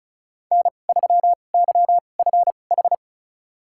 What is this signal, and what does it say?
Morse code, heard as a keyed tone, about 36 wpm, spaced as for 25 wpm, 700 Hz. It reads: N3YFH
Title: USA Call Signs - 5 Characters - 2x Character Spacing - 35wpm